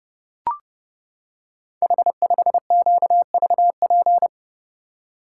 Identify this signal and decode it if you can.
Morse code, heard as a keyed tone, about 30 wpm, 700 Hz. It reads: H5QVP
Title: Code Groups - 5 Letter:Numbers - 30wpm